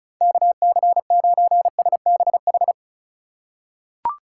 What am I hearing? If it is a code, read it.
Morse code, heard as a keyed tone, about 36 wpm, 690 Hz. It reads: KC9SBH
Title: USA Call Signs - 6 Characters - 3x Character Spacing - 35wpm